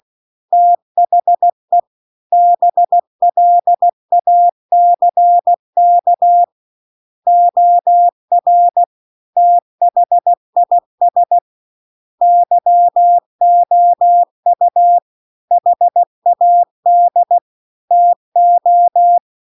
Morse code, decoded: THE BLACK OR THIS YOU HAD TO

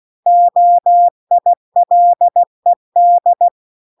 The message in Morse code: OILED

